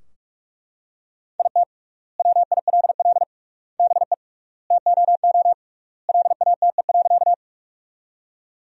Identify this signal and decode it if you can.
Morse code, heard as a keyed tone, about 43 wpm, 700 Hz. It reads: IT WILL BE TOO LATE.
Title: 3x Word-Spacing - Sentences from Top 300 Words - 45wpm